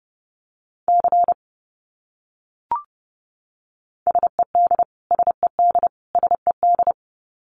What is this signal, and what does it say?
Morse code, heard as a keyed tone, about 30 wpm, 700 Hz. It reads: C SED SED SED
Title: Abbreviations Repeated 3x - 30wpm